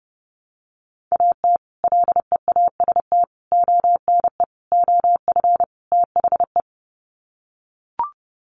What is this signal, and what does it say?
Morse code, heard as a keyed tone, about 30 wpm, 700 Hz. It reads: AT LEAST ONE OF THE
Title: Sets of 5 Words Encore 30wpm